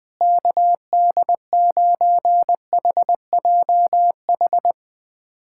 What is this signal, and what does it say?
Morse code, heard as a keyed tone, about 21 wpm, 700 Hz. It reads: KD9HJH
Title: USA Callsign DE Callsign - 6 Characters - 20wpm